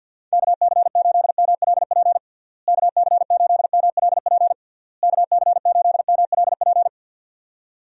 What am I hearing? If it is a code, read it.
Morse code, heard as a keyed tone, about 51 wpm, 700 Hz. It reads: KC8MLP KC8MLP KC8MLP
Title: USA Call Signs - 6 Characters - Repeated 3x - 50wpm